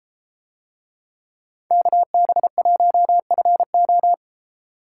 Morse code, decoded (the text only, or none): KB1FO